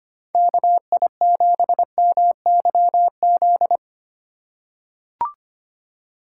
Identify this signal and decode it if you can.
Morse code, heard as a keyed tone, about 25 wpm, 700 Hz. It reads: KI7MYZ
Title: USA Call Signs - 6 Characters - 2x Character Spacing - 25wpm